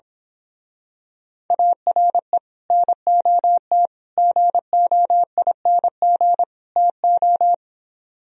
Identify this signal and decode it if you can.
Morse code, heard as a keyed tone, about 26 wpm, 700 Hz. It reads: ARE NOT GOING TO
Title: Speed-Racing - Sets of 4 Words 22wpm